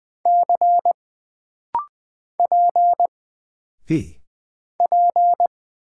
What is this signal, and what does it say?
Morse code, heard as a keyed tone, about 20 wpm, 700 Hz. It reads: C P P
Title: Single Character (Letter:Number:Punctuation:Prosign) - Rapid-Fire - 20wpm